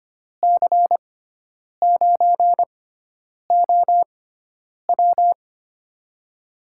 Morse code, heard as a keyed tone, about 25 wpm, spaced as for 9 wpm, 700 Hz. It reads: C9OW
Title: USA Call Signs - 6x Character Spacing - 25wpm